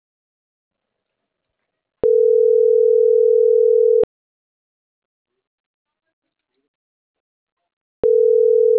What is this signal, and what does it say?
Ringback tone (North America), 440 + 480 Hz.